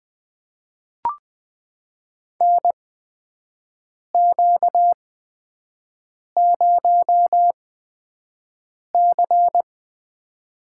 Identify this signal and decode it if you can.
Morse code, heard as a keyed tone, about 21 wpm, spaced as for 5 wpm, 700 Hz. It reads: NQ0C
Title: USA Call Signs - 4 Characters - 8x Character Spacing - 20wpm